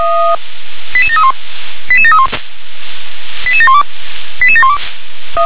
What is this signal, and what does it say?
Five-tone selective call (ZVEI), heard as quick runs of five tones, tones of about 70 ms. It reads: 88731 88731 88712 88712
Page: /357/SDR/ZVEI/